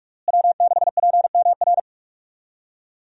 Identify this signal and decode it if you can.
Morse code, heard as a keyed tone, about 46 wpm, 700 Hz. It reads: W6PMR
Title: USA Call Signs - 5 Characters - Repeated 3x - 45wpm